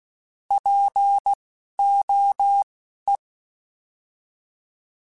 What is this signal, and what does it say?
Morse code, heard as a keyed tone, about 16 wpm, spaced as for 11 wpm, 800 Hz. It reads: POE
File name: Alphabetical / Alphabetical mp3s / Ps @10